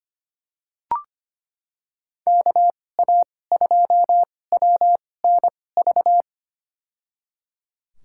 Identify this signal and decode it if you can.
Morse code, heard as a keyed tone, about 25 wpm, spaced as for 18 wpm, 700 Hz. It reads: KA2WNV
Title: USA Call Signs - 2x Character Spacing - 25wpm